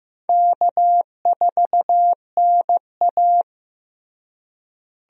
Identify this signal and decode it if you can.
Morse code, heard as a keyed tone, about 15 wpm, 700 Hz. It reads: K4NA
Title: USA Call Signs - 4 Characters - Repeated 2x - 15wpm